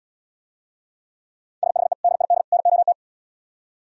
Morse code, heard as a keyed tone, about 77 wpm, 700 Hz. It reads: SHE DID NOT